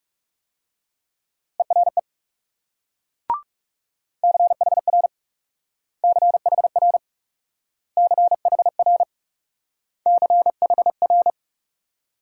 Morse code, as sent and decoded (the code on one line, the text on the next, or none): . .-. / -.-. .... .-. / -.-. .... .-. / -.-. .... .-. / -.-. .... .-.
ERE CHR CHR CHR CHR